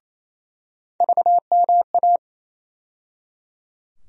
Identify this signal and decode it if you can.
Morse code, heard as a keyed tone, about 27 wpm, 700 Hz. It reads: VMA
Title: USA Call Sign Postfixes - 3 Letters 28wpm